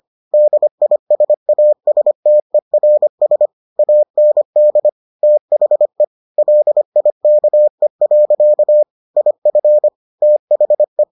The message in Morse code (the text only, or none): DISASTERS AND THE LIKE. IF THE